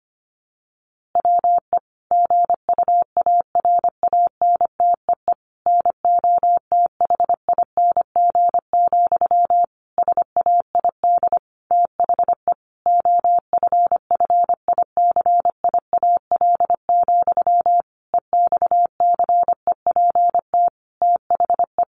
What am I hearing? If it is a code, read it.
Morse code, heard as a keyed tone, about 25 wpm, 700 Hz. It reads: WE GUARANTEE NOTHING, SAID THE OFFICIAL, EXCEPT THE
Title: A Sound Of Thunder 25wpm